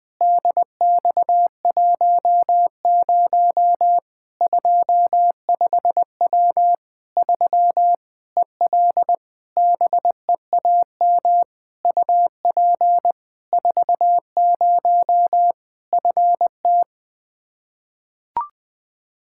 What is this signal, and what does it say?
Morse code, heard as a keyed tone, about 20 wpm, 700 Hz. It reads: DX10 25W 3 EL BEAM UP 40 FT
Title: Practice QSO - Exchange 2 - Info - 20wpm